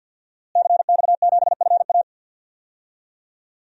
Morse code, heard as a keyed tone, about 51 wpm, 700 Hz. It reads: CX7FA
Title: DXCC-100 Call Signs - 5 Characters - Repeated 3x - 50wpm